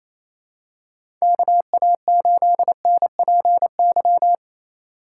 Morse code, heard as a keyed tone, about 28 wpm, 700 Hz. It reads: KA8NPY
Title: USA Call Signs - 4x Character Spacing - 28wpm